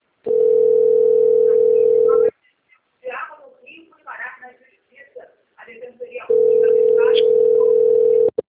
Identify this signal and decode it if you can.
Ringback tone (North America), 440 + 480 Hz, with speech.
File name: rg-1600-unknown- - - .23696